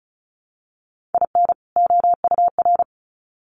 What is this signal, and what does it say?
Morse code, heard as a keyed tone, about 34 wpm, 700 Hz. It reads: IN OUR
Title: Sets of 2 Words 35wpm